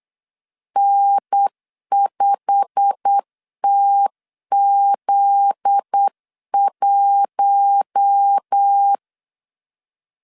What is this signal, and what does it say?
Morse code, heard as a keyed tone, about 8 wpm, 800 Hz. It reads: N5TZ1
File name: CW / DET / 1 9 1500C S A 8wpm 5637